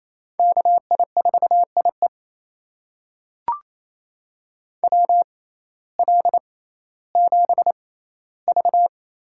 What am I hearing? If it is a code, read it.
Morse code, heard as a keyed tone, about 28 wpm, spaced as for 15 wpm, 700 Hz. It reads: KI4IE WL7V
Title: USA Call Signs - 5 Characters - 6x Character Spacing - 28wpm